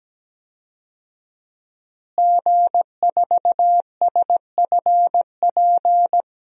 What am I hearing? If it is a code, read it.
Morse code, heard as a keyed tone, about 17 wpm, 700 Hz. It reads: G4SFP
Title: DXCC-100 Call Signs - 5 Characters - Repeated 2x - 17wpm